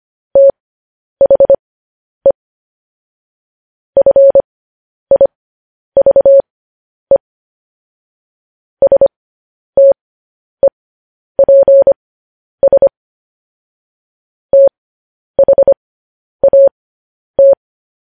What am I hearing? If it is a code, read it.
Morse code, heard as a keyed tone, about 25 wpm, spaced as for 10 wpm, 550 Hz. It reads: THE FIVE STEPS THAT
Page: news-headlines-in-morse-code-at-10-wpm